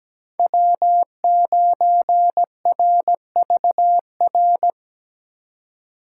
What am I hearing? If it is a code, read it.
Morse code, heard as a keyed tone, about 17 wpm, 700 Hz. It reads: W9RVR
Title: USA Call Signs - 5 Characters - 2x Character Spacing - 17wpm